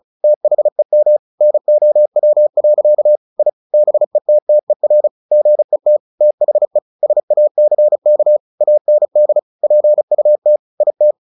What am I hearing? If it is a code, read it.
Morse code, heard as a keyed tone, about 34 wpm, 600 Hz. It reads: THEM NOW. I BETTER GET THE SACK AND PUT IT
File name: the old man and the sea 2 35wpm